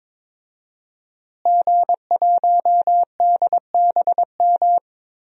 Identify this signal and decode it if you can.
Morse code, heard as a keyed tone, about 22 wpm, 700 Hz. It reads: G1DBM
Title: DXCC-100 Call Signs - 5 Characters - Repeated 3x - 22wpm